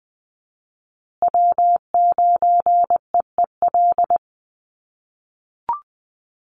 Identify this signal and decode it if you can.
Morse code, heard as a keyed tone, about 20 wpm, 700 Hz. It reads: W9EEL
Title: USA Call Signs Repeated 2x 20wpm